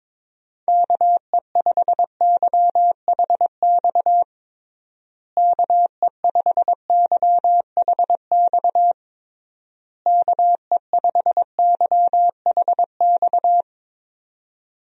Morse code, heard as a keyed tone, about 21 wpm, 700 Hz. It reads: KE5YHX KE5YHX KE5YHX
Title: USA Call Signs - 6 Characters - Repeated 3x - 22wpm